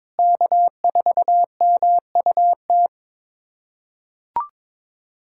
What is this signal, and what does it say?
Morse code, heard as a keyed tone, about 22 wpm, 700 Hz. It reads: K4MUT
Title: USA Callsign DE Callsign - 5 Characters - 22wpm